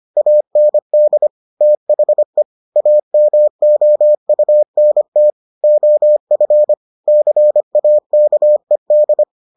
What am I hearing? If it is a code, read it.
Morse code, heard as a keyed tone, about 25 wpm, 600 Hz. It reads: AND THE AMOUNT OF CAKED